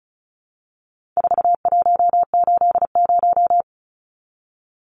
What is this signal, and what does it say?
Morse code, heard as a keyed tone, about 34 wpm, 700 Hz. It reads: 4180
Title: Numbers - 4-Digit 35wpm